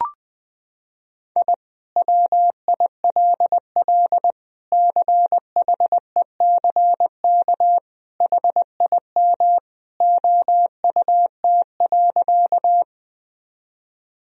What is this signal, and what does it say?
Morse code, heard as a keyed tone, about 21 wpm, 700 Hz. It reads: I WILL CHECK HIM OUT.
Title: Sentences from Top 1000 Words - Review Part 3 - 20wpm